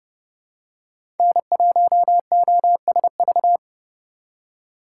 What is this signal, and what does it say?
Morse code, heard as a keyed tone, about 30 wpm, 700 Hz. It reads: N1OSV
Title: USA Call Signs - 5 Characters - 4x Character Spacing - 30wpm